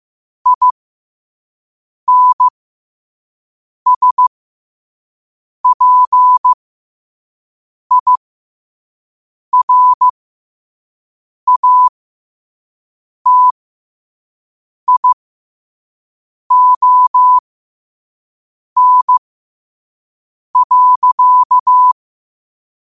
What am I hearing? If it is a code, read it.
Morse code, heard as a keyed tone, about 15 wpm, spaced as for 5 wpm, 1000 Hz. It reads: INSPIRATION.
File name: qotdfiles / 5wpm